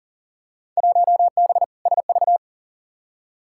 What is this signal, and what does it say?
Morse code, heard as a keyed tone, about 40 wpm, 700 Hz. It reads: 1B SV
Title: Field Day Exchanges - 40wpm